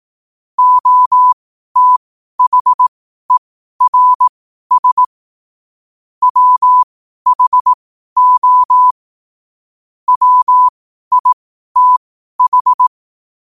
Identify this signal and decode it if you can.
Morse code, heard as a keyed tone, about 18 wpm, spaced as for 12 wpm, 1000 Hz. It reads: OTHERS WHO WITH
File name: qotdfiles / 12wpm